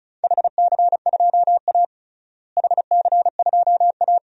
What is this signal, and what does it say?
Morse code, heard as a keyed tone, about 36 wpm, 700 Hz. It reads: HC2A HC2A
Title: DXCC-100 Call Signs - 4 Characters - Repeated 3x - 35wpm